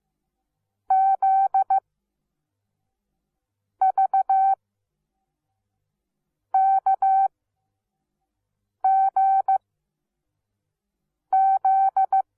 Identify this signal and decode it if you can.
Morse code, heard as a keyed tone, about 15 wpm, spaced as for 4 wpm, 770 Hz. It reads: ZVKGZ
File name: Onlinelehrgang / morsen / DJ4UF Morsen17